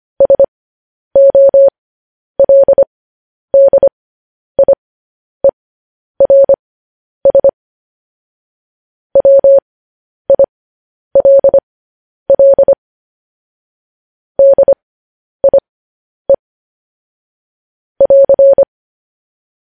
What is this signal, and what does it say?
Morse code, heard as a keyed tone, about 25 wpm, spaced as for 10 wpm, 550 Hz. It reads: SOLDIERS WILL DIE <AR>